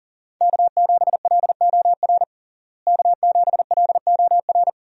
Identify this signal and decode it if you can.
Morse code, heard as a keyed tone, about 37 wpm, 700 Hz. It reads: K7LOR K7LOR
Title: USA Call Signs - 5 Characters - Repeated 2x - 40wpm